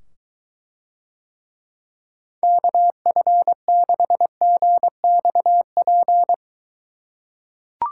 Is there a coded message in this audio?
Yes: Morse code from 2.4 to 6.3 s, heard as a keyed tone, about 23 wpm, 700 Hz. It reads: KF6GXP